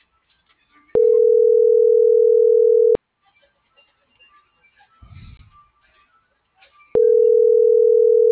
Ringback tone (North America), 440 + 480 Hz.